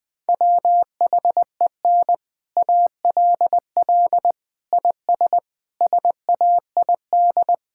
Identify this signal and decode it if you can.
Morse code, heard as a keyed tone, about 20 wpm, 700 Hz. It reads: WHEN ALL IS SAID
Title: Sets of 6 Words - 20wpm